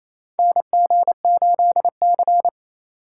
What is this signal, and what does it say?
Morse code, heard as a keyed tone, about 27 wpm, 700 Hz. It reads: NG8C
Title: USA Call Signs - 3x Character Spacing - 28wpm